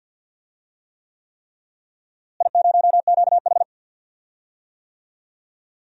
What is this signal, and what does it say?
Morse code, heard as a keyed tone, about 51 wpm, 700 Hz. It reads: I0QH
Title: DXCC-100 Call Signs - 4 Characters - Repeated 3x - 50wpm